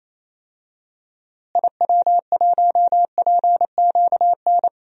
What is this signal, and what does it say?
Morse code, heard as a keyed tone, about 28 wpm, 700 Hz. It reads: IW1PQN
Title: DXCC-100 Call Signs - 6 Characters - Repeated 2x - 28wpm